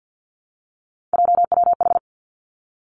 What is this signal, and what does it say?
Morse code, heard as a keyed tone, about 50 wpm, 700 Hz. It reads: JWH
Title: Three Challenging Letter-Number Combinations 50wpm